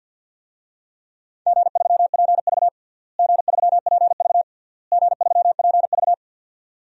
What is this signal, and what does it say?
Morse code, heard as a keyed tone, about 49 wpm, 700 Hz. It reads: G3PV G3PV G3PV
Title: DXCC-100 Call Signs - 4 Characters - Repeated 3x - 50wpm